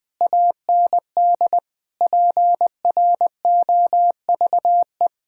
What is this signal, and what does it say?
Morse code, heard as a keyed tone, about 20 wpm, 700 Hz. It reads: AND PROVE
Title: Sentences with 8-Word Phrases - 20wpm